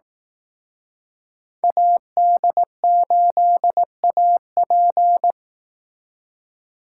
Morse code, heard as a keyed tone, about 18 wpm, 700 Hz. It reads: AD8AP
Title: Speed Racing - USA 2x2 Call Signs 15wpm